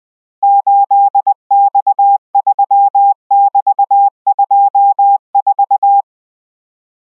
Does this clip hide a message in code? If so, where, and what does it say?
Yes: Morse code from 0.4 to 6.0 s, heard as a keyed tone, about 20 wpm, 800 Hz. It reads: 8X3<BT>24